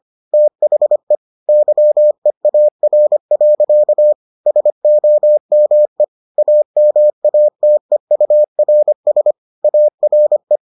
Morse code, heard as a keyed tone, about 25 wpm, 600 Hz. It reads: THE YEAR. SOME AMATEURS ARE